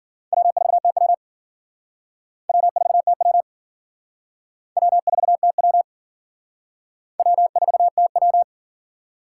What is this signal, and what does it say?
Morse code, heard as a keyed tone, about 52 wpm, 700 Hz. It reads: W4TW W4TW W4TW W4TW